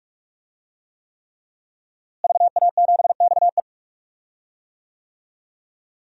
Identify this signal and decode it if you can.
Morse code, heard as a keyed tone, about 46 wpm, 700 Hz. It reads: VA7XE
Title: DXCC-100 Call Signs - 5 Characters - Repeated 3x - 45wpm